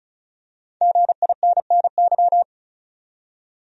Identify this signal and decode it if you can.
Morse code, heard as a keyed tone, about 36 wpm, 700 Hz. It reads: GINNY
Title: USA Common Names - 35wpm